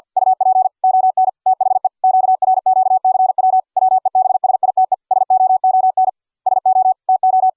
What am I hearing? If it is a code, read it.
Morse code, heard as a keyed tone, about 48 wpm, 750 Hz. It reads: UP ON THE QRQCW WEBSITE SOON SO TO